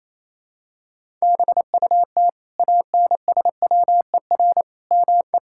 Morse code, heard as a keyed tone, about 28 wpm, 700 Hz. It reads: BUT ANSWER ME
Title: Sentences from Top 1000 Words - Review Part 5 - 28wpm